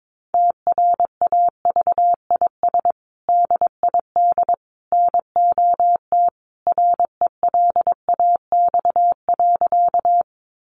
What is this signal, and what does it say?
Morse code, heard as a keyed tone, about 22 wpm, 700 Hz. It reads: TRAVIS DID NOT RELAX.
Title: A Sound Of Thunder 22wpm